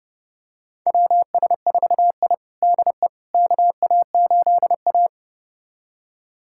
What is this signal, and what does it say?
Morse code, heard as a keyed tone, about 32 wpm, 700 Hz. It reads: WS4I DE KA8A
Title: USA Callsign DE Callsign - 4 Characters - 30wpm